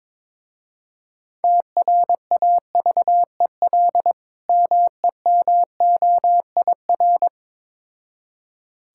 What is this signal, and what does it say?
Morse code, heard as a keyed tone, about 22 wpm, 690 Hz. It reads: TRAVEL MEMOIR
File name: Difficult Words with Hints - 22wpm